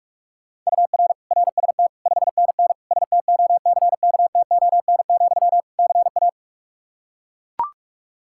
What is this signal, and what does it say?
Morse code, heard as a keyed tone, about 41 wpm, 700 Hz. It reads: UR RST 5NN STOCKTON, CA
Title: K1USN SST QSO Practice 45wpm